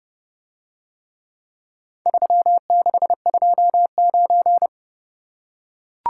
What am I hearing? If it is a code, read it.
Morse code, heard as a keyed tone, about 29 wpm, 700 Hz. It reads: 3629